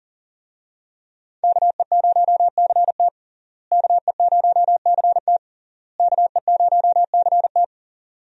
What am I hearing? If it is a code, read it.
Morse code, heard as a keyed tone, about 40 wpm, 700 Hz. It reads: KE0CT KE0CT KE0CT
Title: USA Call Signs - 5 Characters - Repeated 3x - 40wpm